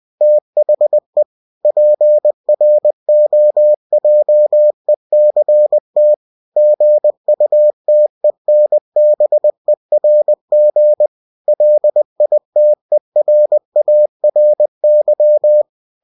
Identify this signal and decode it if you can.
Morse code, heard as a keyed tone, about 20 wpm, 600 Hz. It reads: THE PROJECT GUTENBERG LITERARY
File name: cw-books / winnie-the-pooh / winnie-20wpm-page-0120